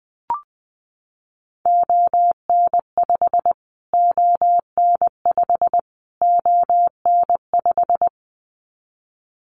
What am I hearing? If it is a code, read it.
Morse code, heard as a keyed tone, about 20 wpm, 700 Hz. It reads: ON5 ON5 ON5
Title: Callsign Prefixes for Top 25 DXCC - Repeated 3x - 20wpm